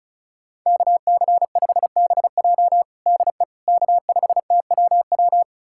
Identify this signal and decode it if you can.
Morse code, heard as a keyed tone, about 34 wpm, 690 Hz. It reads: KC5BJ DE K5TWW
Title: USA Callsign DE Callsign - 5 Characters - 35wpm